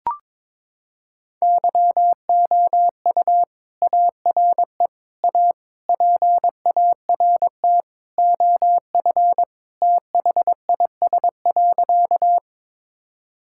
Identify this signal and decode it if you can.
Morse code, heard as a keyed tone, about 22 wpm, 700 Hz. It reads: YOU ARE A PART OF THIS.
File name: practice / 22wpm / Sentences with 3-Word Phrases - 22wpm